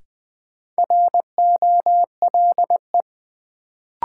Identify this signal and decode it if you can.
Morse code, heard as a keyed tone, about 20 wpm, 700 Hz. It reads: ROLE
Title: Sentences from Top 1500 Words - New Word Drill Alt - 20wpm